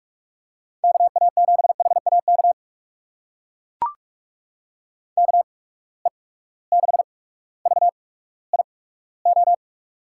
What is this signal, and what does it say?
Morse code, heard as a keyed tone, about 43 wpm, spaced as for 19 wpm, 700 Hz. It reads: KA7HAK KE6VIO
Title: USA Call Signs - 6 Characters - 8x Character Spacing - 45wpm